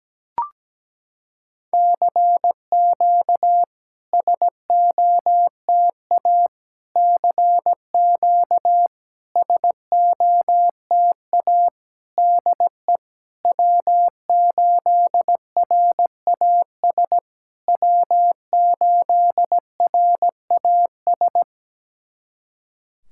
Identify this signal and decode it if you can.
Morse code, heard as a keyed tone, about 17 wpm, 700 Hz. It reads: CQ SOTA CQ SOTA DE W8RAS W8RAS
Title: SOTA QSO Practice 17wpm